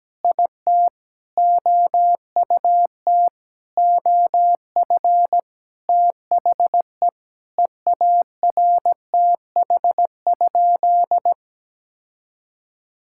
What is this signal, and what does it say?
Morse code, heard as a keyed tone, about 17 wpm, 700 Hz. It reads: IT OUT OF THE EARTH?
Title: Sentences from Top 1600 Words 17wpm